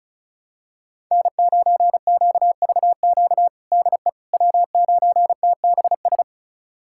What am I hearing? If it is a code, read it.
Morse code, heard as a keyed tone, about 36 wpm, 700 Hz. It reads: N9QVQ DE W9TBS
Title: USA Callsign DE Callsign - 5 Characters - 35wpm